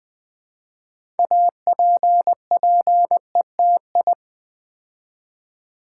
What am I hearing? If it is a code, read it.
Morse code, heard as a keyed tone, about 20 wpm, 700 Hz. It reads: APPETI
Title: Build A Word - 2 Letters At A Time - 8-Letter Words - 20wpm